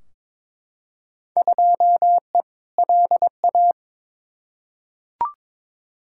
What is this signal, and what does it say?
Morse code, heard as a keyed tone, about 22 wpm, 700 Hz. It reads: 2E LA
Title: Field Day Exchanges Encore - 22wpm